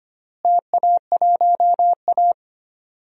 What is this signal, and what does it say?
Morse code, heard as a keyed tone, about 25 wpm, 700 Hz. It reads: TA1A
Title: DXCC-100 Call Signs - 4 Characters - Repeated 2x - 25wpm